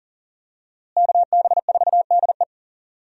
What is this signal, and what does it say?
Morse code, heard as a keyed tone, about 40 wpm, 700 Hz. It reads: KB4DE
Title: USA Call Signs - 5 Characters - Repeated 3x - 40wpm